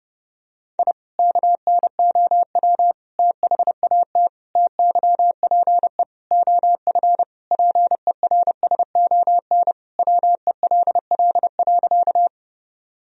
Morse code, heard as a keyed tone, about 32 wpm, 700 Hz. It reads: I KNOW THAT TYPE OF PERSON WELL.
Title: Sentences from Top 1000 Words - Review Part 3 - 30wpm